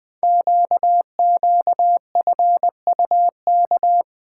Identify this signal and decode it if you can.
Morse code, heard as a keyed tone, about 20 wpm, 700 Hz. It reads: QQFUK
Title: Code Groups - 5 Letters - 20wpm